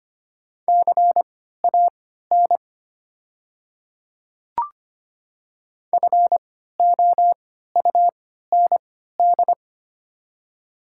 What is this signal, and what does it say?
Morse code, heard as a keyed tone, about 25 wpm, spaced as for 14 wpm, 700 Hz. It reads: CAN FOUND
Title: 3x Character Spacing - Top 300 Words - 25wpm